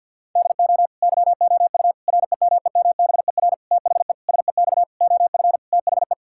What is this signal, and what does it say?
Morse code, heard as a keyed tone, about 50 wpm, 700 Hz. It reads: DO YOU REMEMBER THE SEX OF THE
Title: Sentences from Top 1600 Words 50wpm